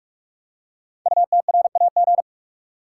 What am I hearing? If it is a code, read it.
Morse code, heard as a keyed tone, about 44 wpm, 700 Hz. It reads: UTRAG
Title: Code Groups - 5 Letter:Numbers - 45wpm